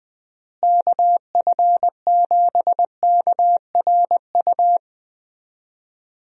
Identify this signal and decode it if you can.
Morse code, heard as a keyed tone, about 21 wpm, 700 Hz. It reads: KF7KRU
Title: USA Call Signs - 6 Characters - Repeated 3x - 20wpm